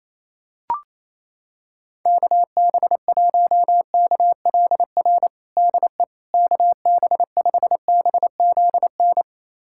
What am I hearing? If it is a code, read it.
Morse code, heard as a keyed tone, about 28 wpm, 700 Hz. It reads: KB1KLR DE KB5BZN
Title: USA Callsign DE Callsign - 6 Characters - 28wpm